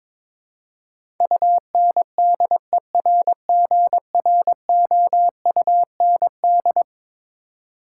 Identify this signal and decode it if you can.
Morse code, heard as a keyed tone, about 21 wpm, 700 Hz. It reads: UNDERGROUND
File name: practice / 22wpm / Eleven Letter Words - 22wpm